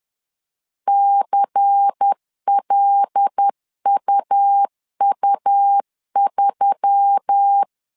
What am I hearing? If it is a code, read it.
Morse code, heard as a keyed tone, about 10 wpm, 800 Hz. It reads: CLUU3